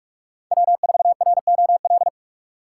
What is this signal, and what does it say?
Morse code, heard as a keyed tone, about 44 wpm, 700 Hz. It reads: W4ROL